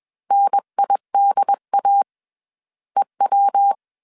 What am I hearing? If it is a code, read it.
Morse code, heard as a keyed tone, about 21 wpm, 800 Hz. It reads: NIDA EW